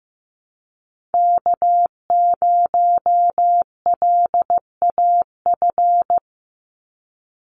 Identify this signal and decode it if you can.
Morse code, heard as a keyed tone, about 15 wpm, 700 Hz. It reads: K0LAF
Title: NA SOTA Chaser Callsigns 15wpm